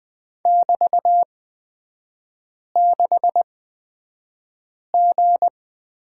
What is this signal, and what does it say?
Morse code, heard as a keyed tone, about 20 wpm, spaced as for 5 wpm, 700 Hz. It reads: <BT>6G
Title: 224 Lesson 36e - BT - Three Characters 20wpm